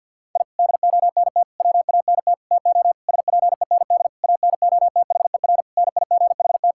Morse code, heard as a keyed tone, about 49 wpm, 700 Hz. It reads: I DONT WANT TO SPEND ANOTHER NIGHT
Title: Sentences from Top 1000 Words - Review Part 5 - 50wpm